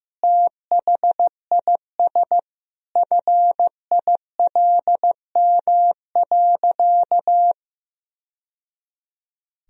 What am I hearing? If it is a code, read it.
Morse code, heard as a keyed tone, about 15 wpm, 700 Hz. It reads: THIS FILM.
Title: Sentences from Top 1000 Words - Review Part 3 - 15wpm